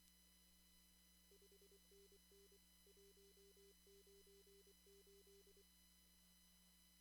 Morse code, heard as a keyed tone, about 24 wpm, 400 Hz, with noise. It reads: 5NN 198
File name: CW / kt5j wpx cw-op-k5pi